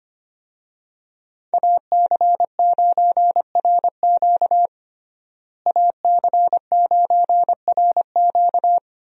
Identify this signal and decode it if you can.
Morse code, heard as a keyed tone, about 25 wpm, 700 Hz. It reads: AC9RQ AC9RQ
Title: USA Call Signs - 5 Characters - Repeated 2x - 25wpm